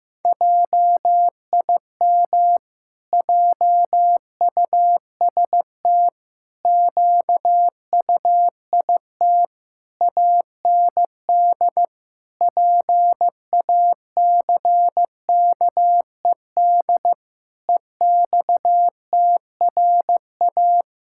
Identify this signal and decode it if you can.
Morse code, heard as a keyed tone, about 16 wpm, 700 Hz. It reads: JIM JUST QUIT AND PACKED EXTRA
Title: Pangram Challenge v2 - 15wpm